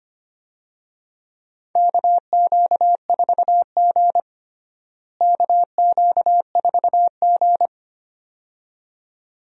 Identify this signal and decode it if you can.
Morse code, heard as a keyed tone, about 25 wpm, 700 Hz. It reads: KQ4G KQ4G
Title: USA Call Signs - 4 Characters - Repeated 2x - 25wpm